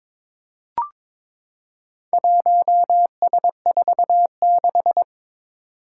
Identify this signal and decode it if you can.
Morse code, heard as a keyed tone, about 22 wpm, 700 Hz. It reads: 1S46
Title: Four Challenging Letter-Number Combinations 22wpm